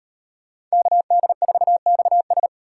Morse code, heard as a keyed tone, about 38 wpm, 700 Hz. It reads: KD4XS